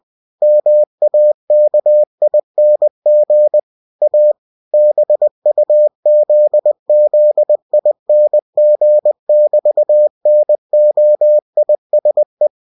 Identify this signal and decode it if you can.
Morse code, heard as a keyed tone, about 20 wpm, 600 Hz. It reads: MAKING A BUZZING<BT>NOISE
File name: cw-books / winnie-the-pooh / winnie-20wpm-page-0002